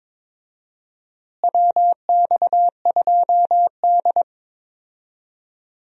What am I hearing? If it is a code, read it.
Morse code, heard as a keyed tone, about 22 wpm, 700 Hz. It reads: WX2D
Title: USA Call Signs - 4 Characters - Repeated 2x - 22wpm